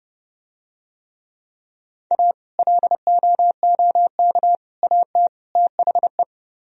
Morse code, heard as a keyed tone, about 30 wpm, 700 Hz. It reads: A LOOK AT THE